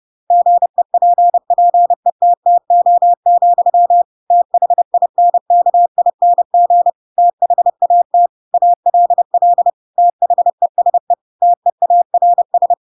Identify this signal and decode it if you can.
Morse code, heard as a keyed tone, about 30 wpm, 700 Hz. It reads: GEPPETTO, THINKING THAT ALL THESE TEARS
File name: Pinocchio / Pinocchio ch7 30 30 ik1hge